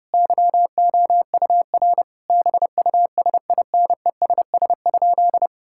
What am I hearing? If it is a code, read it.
Morse code, heard as a keyed tone, about 30 wpm, 700 Hz. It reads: YOUR BUSINESS?
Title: Sentences from Top 1000 Words - Review Part 5 - 30wpm